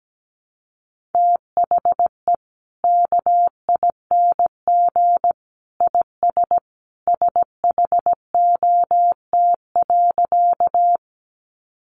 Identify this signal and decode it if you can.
Morse code, heard as a keyed tone, about 17 wpm, 700 Hz. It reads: THE KING IS SHOT.
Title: Sentences from Top 1100 Words 17wpm